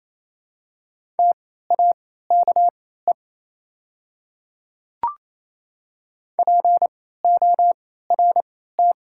Morse code, heard as a keyed tone, about 28 wpm, spaced as for 16 wpm, 700 Hz. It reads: TAKE PORT
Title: 3x Character Spacing - Top 200 Words - 28wpm